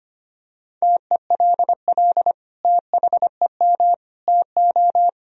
Morse code, heard as a keyed tone, about 25 wpm, 700 Hz. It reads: TELL THEM TO
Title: Sentences from Top 1000 Words - Review Part 3 - 25wpm